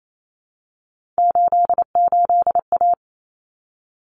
Morse code, heard as a keyed tone, about 28 wpm, 700 Hz. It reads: 88A
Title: Common Cut Numbers - 3-Digit 28wpm